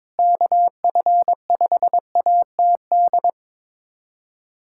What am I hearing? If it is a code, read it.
Morse code, heard as a keyed tone, about 22 wpm, 700 Hz. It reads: KF5ATD
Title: USA Callsign DE Callsign - 6 Characters - 22wpm